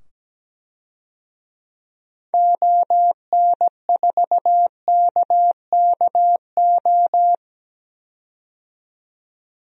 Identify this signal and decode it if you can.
Morse code, heard as a keyed tone, about 17 wpm, 700 Hz. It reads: ON4KKO
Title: DXCC-100 Call Signs - 6 Characters - 17wpm